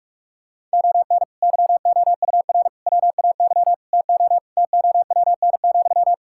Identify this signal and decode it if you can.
Morse code, heard as a keyed tone, about 46 wpm, 700 Hz. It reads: ON YOUR WAY TO TOWN,
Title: Sentences from Top 1000 Words - Review Part 6 - 45wpm